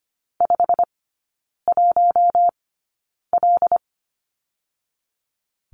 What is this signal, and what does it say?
Morse code, heard as a keyed tone, about 24 wpm, spaced as for 9 wpm, 690 Hz. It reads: 51L
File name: Lesson 15e - Number 5 - Three Characters 25wpm